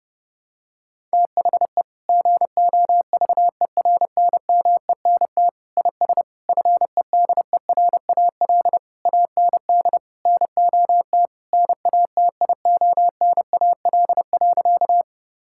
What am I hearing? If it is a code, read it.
Morse code, heard as a keyed tone, about 30 wpm, 700 Hz. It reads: THE GOVERNMENT IS FEDERAL AND NOT NATIONAL.
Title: Sentences from Top 1000 Words - Review Part 5 - 30wpm